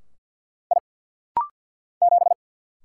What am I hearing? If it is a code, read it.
Morse code, heard as a keyed tone, about 49 wpm, 690 Hz. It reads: I 7
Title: Single Letter-Number - Rapid-Fire 50wpm